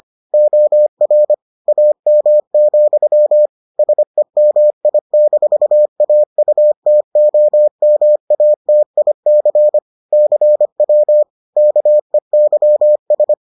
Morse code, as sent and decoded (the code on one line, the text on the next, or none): --- .-. / .- -- --..-- / ... -- .. -....- .- ..- - --- -- .- - .. -.-. / -.-. .-- / -.- . -.-- ...
OR AM, SEMI-AUTOMATIC CW KEYS